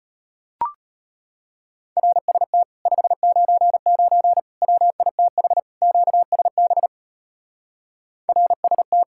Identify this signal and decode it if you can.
Morse code, heard as a keyed tone, about 33 wpm, 700 Hz. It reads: RST 599 WITH QSB RST